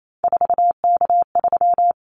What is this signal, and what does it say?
Morse code, heard as a keyed tone, about 28 wpm, 700 Hz. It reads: 4K3
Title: Callsign Prefixes for Top 100 DXCC - 28wpm